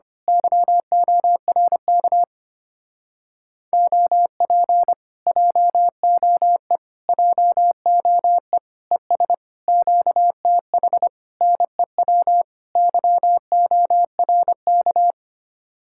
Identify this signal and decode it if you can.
Morse code, heard as a keyed tone, about 25 wpm, 700 Hz. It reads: YORK OP JOE JOE ES QTH NEW YORK
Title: Speed-Racing Common QSO Phrases - 25wpm